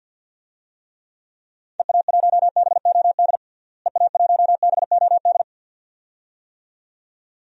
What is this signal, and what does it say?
Morse code, heard as a keyed tone, about 47 wpm, 700 Hz. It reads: EA1BOD EA1BOD
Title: DXCC-100 Call Signs - 6 Characters - Repeated 2x - 50wpm